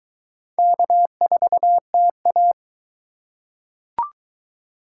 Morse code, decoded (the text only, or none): K4TA